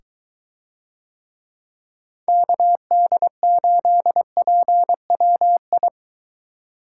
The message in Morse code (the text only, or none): KD8PWI